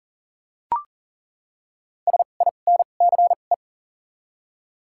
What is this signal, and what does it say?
Morse code, heard as a keyed tone, about 37 wpm, 700 Hz. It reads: S I N C E
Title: 2x Character Spacing - Top 300 Words - 40wpm